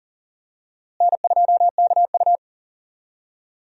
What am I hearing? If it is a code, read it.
Morse code, heard as a keyed tone, about 40 wpm, 700 Hz. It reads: N2KU
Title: USA Call Signs - 4 Characters - Repeated 2x - 40wpm